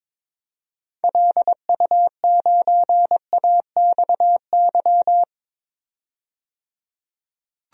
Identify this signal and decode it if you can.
Morse code, heard as a keyed tone, about 22 wpm, 700 Hz. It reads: LU9AXY
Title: DXCC-100 Call Signs - 6 Characters - 22wpm